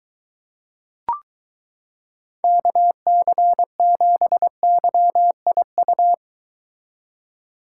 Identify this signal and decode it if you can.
Morse code, heard as a keyed tone, about 23 wpm, 700 Hz. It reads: KC7YIU